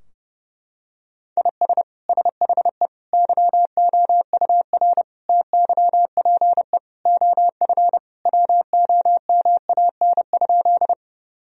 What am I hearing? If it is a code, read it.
Morse code, heard as a keyed tone, about 30 wpm, 700 Hz. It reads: IS SHE YOUR TYPE OF WOMAN?
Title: Sentences from Top 1000 Words - Review Part 3 - 30wpm